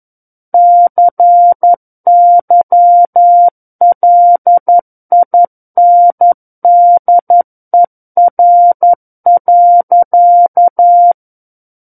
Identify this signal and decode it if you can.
Morse code, heard as a keyed tone, about 11 wpm, 700 Hz. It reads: CYLINDER.
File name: Book One / 02-Chapter 2 - 11 WPM